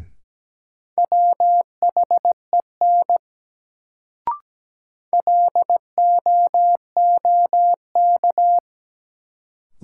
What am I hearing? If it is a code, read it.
Morse code, heard as a keyed tone, about 17 wpm, 700 Hz. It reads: WHEN LOOK
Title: Sets of 2 Words - New Word Drill - 17wpm